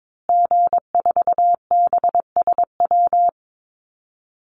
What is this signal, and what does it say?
Morse code, heard as a keyed tone, about 22 wpm, 700 Hz. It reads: G4BSW
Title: Call Signs Repeated 2x - 22wpm